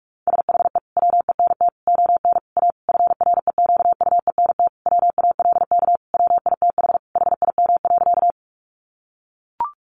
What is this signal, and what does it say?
Morse code, heard as a keyed tone, about 44 wpm, 700 Hz. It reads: SHE WENT ON A FREQUENT WALK WITH HIM.
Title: Sentences from Top 1400 Words 45wpm